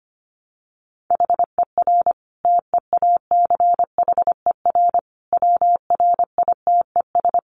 Morse code, heard as a keyed tone, about 25 wpm, 700 Hz. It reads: HER TEACHER WRITES
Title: Additional Easy Phrases 25wpm